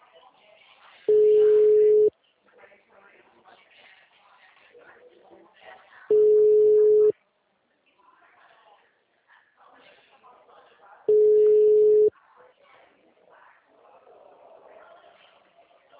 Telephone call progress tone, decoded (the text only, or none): ringback tone